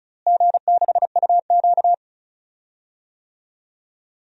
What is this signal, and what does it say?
Morse code, heard as a keyed tone, about 34 wpm, 700 Hz. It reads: G6UQ
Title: DXCC-100 Call Signs - 4 Characters - Repeated 3x - 35wpm